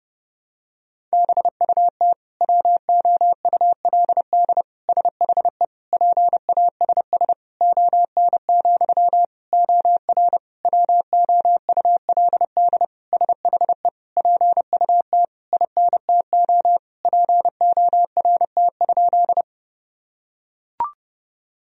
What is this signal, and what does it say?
Morse code, heard as a keyed tone, about 30 wpm, 700 Hz. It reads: BUT WOULD SHE PASS ON, OR WOULD SHE PUT INTO PORT?
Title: Sentences from Top 1000 Words - Review Part 1 - 30wpm